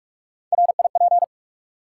Morse code, heard as a keyed tone, about 48 wpm, 700 Hz. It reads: RIP
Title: Three Letter Words - 45wpm